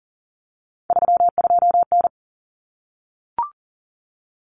Morse code, heard as a keyed tone, about 40 wpm, 700 Hz. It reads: 32N
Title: Common Cut Numbers - 3-Digit 40wpm